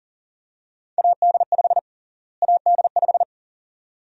Morse code, heard as a keyed tone, about 37 wpm, 700 Hz. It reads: AD5 AD5
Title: USA Call Sign Prefixes - Repeated 2x - 40wpm